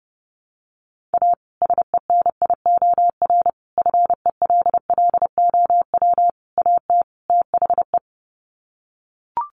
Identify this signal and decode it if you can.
Morse code, heard as a keyed tone, about 28 wpm, 700 Hz. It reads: A SENIOR FELLOW AT THE
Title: Sets of 5 Words Encore 30wpm